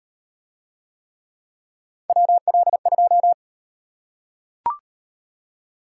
Morse code, heard as a keyed tone, about 34 wpm, 700 Hz. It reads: WL2